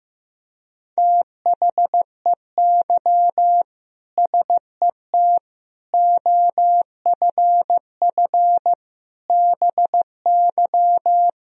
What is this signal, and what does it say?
Morse code, heard as a keyed tone, about 15 wpm, 700 Hz. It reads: THEY SET OFF BY
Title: Sentences from Top 1000 Words - Review Part 3 - 15wpm